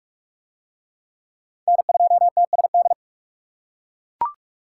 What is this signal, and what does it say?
Morse code, heard as a keyed tone, about 39 wpm, 700 Hz. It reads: N2TSD